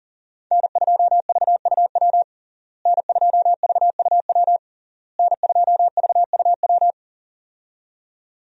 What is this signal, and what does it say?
Morse code, heard as a keyed tone, about 40 wpm, 700 Hz. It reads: N2VUW N2VUW N2VUW
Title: USA Call Signs - 5 Characters - Repeated 3x - 40wpm